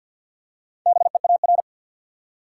Morse code, heard as a keyed tone, about 52 wpm, 700 Hz. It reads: BEAR
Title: Four Letter Words - 50wpm